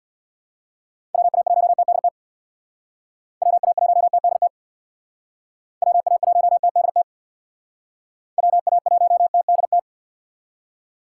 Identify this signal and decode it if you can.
Morse code, heard as a keyed tone, about 62 wpm, 700 Hz. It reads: WA1TDT WA1TDT WA1TDT WA1TDT
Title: Speed Racing - USA 2x3 Call Signs 50wpm